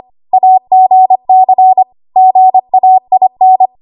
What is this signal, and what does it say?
Morse code, heard as a keyed tone, about 25 wpm, 750 Hz. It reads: AGC GAIN
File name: Archive / 25 WPM / 25WPM